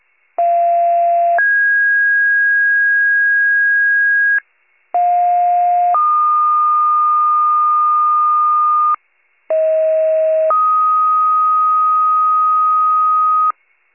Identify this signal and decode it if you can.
Two-tone page, heard as a one-second tone, then a three-second tone, about 690 then 1650 Hz, about 710 then 1220 Hz, about 640 then 1250 Hz, with noise. It is heard from 0.4 to 13.5 s.
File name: pager / 38 FIRE / wcalert